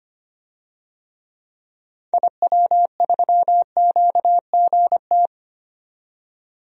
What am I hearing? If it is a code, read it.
Morse code, heard as a keyed tone, about 25 wpm, 700 Hz. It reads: IW3QGT
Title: DXCC-100 Call Signs - 6 Characters - 25wpm